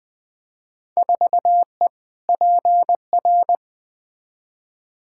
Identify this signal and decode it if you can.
Morse code, heard as a keyed tone, about 20 wpm, 700 Hz. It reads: 4E PR
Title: Field Day Exchanges Encore - 20wpm